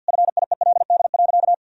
Morse code, heard as a keyed tone, about 50 wpm, 700 Hz. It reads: FIELD.
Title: Sentences from Top 1000 Words - Review Part 2 - 50wpm